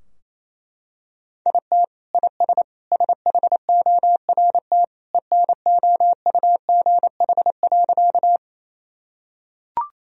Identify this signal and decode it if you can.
Morse code, heard as a keyed tone, about 27 wpm, 700 Hz. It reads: IT IS SHORT ENOUGH.
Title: Sentences from Top 1000 Words - Review Part 2 - 28wpm